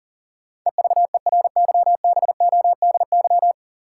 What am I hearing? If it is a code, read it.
Morse code, heard as a keyed tone, about 40 wpm, 700 Hz. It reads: EVERYBODY
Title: Sentences from Top 1400 Words - New Word Drill - 40wpm